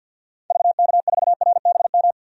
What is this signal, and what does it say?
Morse code, heard as a keyed tone, about 50 wpm, 700 Hz. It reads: VK4RBM
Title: DXCC-100 Call Signs - 6 Characters - 50wpm